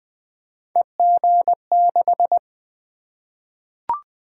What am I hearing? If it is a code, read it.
Morse code, heard as a keyed tone, about 20 wpm, 700 Hz. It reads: EG6
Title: Three Letter-Number Combinations - 20wpm